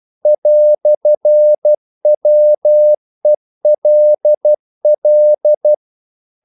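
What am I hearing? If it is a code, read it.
Morse code, heard as a keyed tone, about 12 wpm, 600 Hz. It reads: "WELL
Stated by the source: Morse practice sending English quotations.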